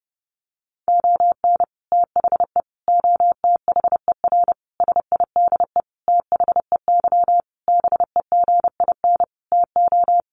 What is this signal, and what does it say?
Morse code, heard as a keyed tone, about 30 wpm, 700 Hz. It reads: ON THE OTHER SIDE THEY BEGIN TO
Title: Sentences from Top 1200 Words 30wpm